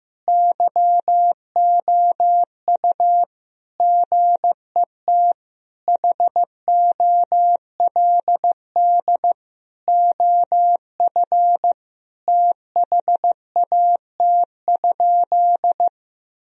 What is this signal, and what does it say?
Morse code, heard as a keyed tone, about 15 wpm, 700 Hz. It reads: YOU GET HOLD OF THAT?
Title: Sentences from Top 1000 Words - Review Part 2 - 15wpm